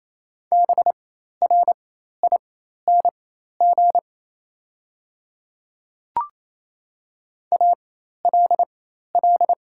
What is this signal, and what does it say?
Morse code, heard as a keyed tone, about 27 wpm, spaced as for 13 wpm, 700 Hz. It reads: BRING ALL
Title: 4x Character Spacing - Top 500 Words - 28wpm